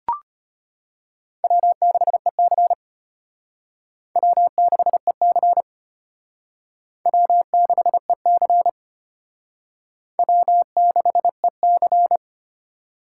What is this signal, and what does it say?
Morse code, heard as a keyed tone, about 32 wpm, 700 Hz. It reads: W6EC W6EC W6EC W6EC